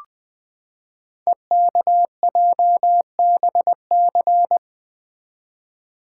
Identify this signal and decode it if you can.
Morse code, heard as a keyed tone, about 20 wpm, 700 Hz. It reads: EKJBC
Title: Code Groups - 5 Letters - 20wpm